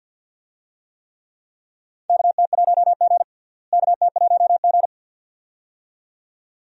Morse code, heard as a keyed tone, about 48 wpm, 700 Hz. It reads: KT1G KT1G
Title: USA Call Signs - 4 Characters - Repeated 2x - 50wpm